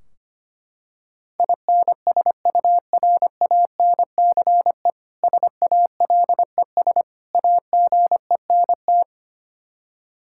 Morse code, heard as a keyed tone, about 25 wpm, 700 Hz. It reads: INSURANCE SALES AGENT
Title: K1USN SST QSO Practice 25wpm